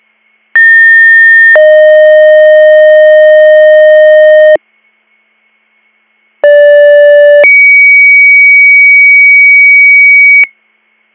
Two-tone page, heard as a one-second tone, then a three-second tone, about 1800 then 620 Hz, about 580 then 2250 Hz. It is heard from 0.5 to 10.4 s.